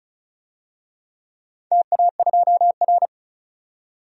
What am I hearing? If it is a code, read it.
Morse code, heard as a keyed tone, about 36 wpm, 690 Hz. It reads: TA2R
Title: DXCC-100 Call Signs - 4 Characters - 35wpm